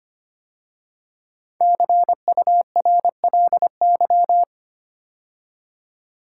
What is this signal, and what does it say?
Morse code, heard as a keyed tone, about 25 wpm, 700 Hz. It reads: CURLY
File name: 5 Letter Words - First 3 Letters Given - 25wpm